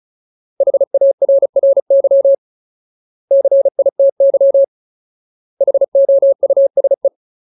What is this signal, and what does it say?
Morse code, heard as a keyed tone, about 34 wpm, 550 Hz. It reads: HARRY CITY HOUSE
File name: Ultimate Word List / 35WPM Ultimate